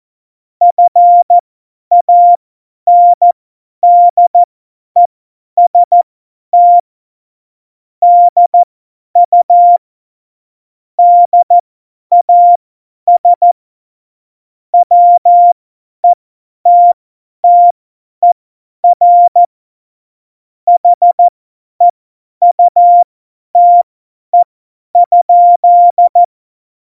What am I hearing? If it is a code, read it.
Morse code, heard as a keyed tone, about 14 wpm, spaced as for 10 wpm, 700 Hz. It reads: FANDEST DU DAS WETTER HEUTE?
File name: Morsekurs / Witze / Witz 14. November 2017, 50 F70